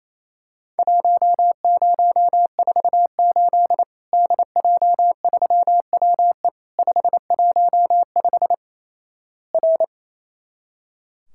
Morse code, heard as a keyed tone, about 27 wpm, 700 Hz. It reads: 1048 DJ3WE 515 R
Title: Worked All Europe DX Contest - QTC Exchange - 28wpm